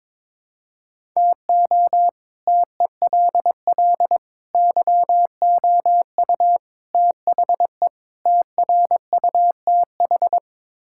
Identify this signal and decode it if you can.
Morse code, heard as a keyed tone, about 22 wpm, 700 Hz. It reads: TO TELL YOU THE TRUTH
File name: practice / 22wpm / Build A Phrase - 5-Word Phrases - 22wpm